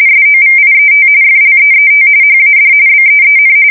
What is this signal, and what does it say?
RTTY, about 45.45 baud. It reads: SOVIET FLEET HAD BEEN
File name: fsk170 / usnavy5